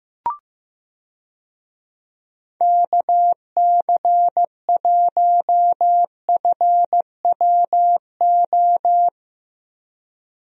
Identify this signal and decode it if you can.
Morse code, heard as a keyed tone, about 15 wpm, 700 Hz. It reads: KC1FWO
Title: USA Call Signs - 6 Characters - Repeated 3x - 15wpm